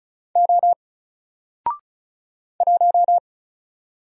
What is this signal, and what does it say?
Morse code, heard as a keyed tone, about 34 wpm, 700 Hz. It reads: O 1
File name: Single-Letter-Number-35wpm